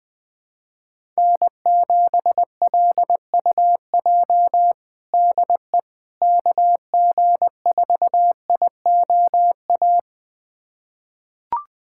Morse code, heard as a keyed tone, about 20 wpm, 700 Hz. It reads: N7LUJ DE KG4IOA
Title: USA Callsign DE Callsign 20wpm